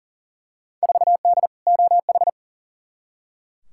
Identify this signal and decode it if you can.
Morse code, heard as a keyed tone, about 40 wpm, 700 Hz. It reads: 4D OH
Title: Field Day Exchanges Encore - 40wpm